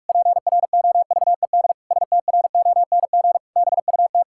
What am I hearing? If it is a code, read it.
Morse code, heard as a keyed tone, about 48 wpm, 700 Hz. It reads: PROVED STRONG BUT
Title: Pangram Challenge v2 - 45wpm